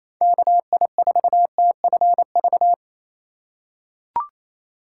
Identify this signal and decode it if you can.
Morse code, heard as a keyed tone, about 28 wpm, 700 Hz. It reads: KI4TFV
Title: USA Call Signs - 6 Characters - 3x Character Spacing - 28wpm